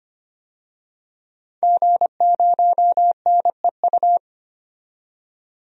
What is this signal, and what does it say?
Morse code, heard as a keyed tone, about 25 wpm, 700 Hz. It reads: G0NEU
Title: DXCC-100 Call Signs - 5 Characters - 25wpm